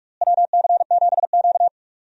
Morse code, heard as a keyed tone, about 45 wpm, 700 Hz. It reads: WC7Q
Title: USA Call Signs - 4 Characters - Repeated 2x - 45wpm